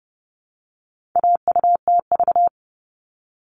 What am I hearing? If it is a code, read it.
Morse code, heard as a keyed tone, about 30 wpm, 700 Hz. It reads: AUTV
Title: All Cut Numbers - 4-Digit 30wpm